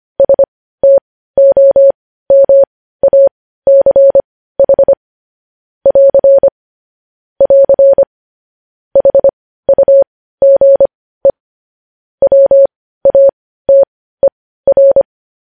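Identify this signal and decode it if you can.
Morse code, heard as a keyed tone, about 25 wpm, spaced as for 15 wpm, 550 Hz. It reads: STOMACH <AR> <AR> HUGE WATER
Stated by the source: Morse code news headlines at 15 wpm.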